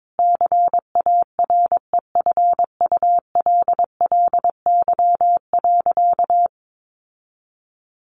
Morse code, heard as a keyed tone, about 22 wpm, 700 Hz. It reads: CAREFULLY.
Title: Sentences from Top 1500 Words 22wpm